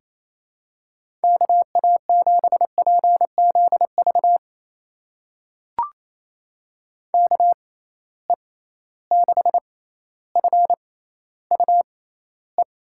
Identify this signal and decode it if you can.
Morse code, heard as a keyed tone, about 28 wpm, spaced as for 14 wpm, 700 Hz. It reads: KA7PZV KE6FUE
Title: USA Call Signs - 6 Characters - 6x Character Spacing - 28wpm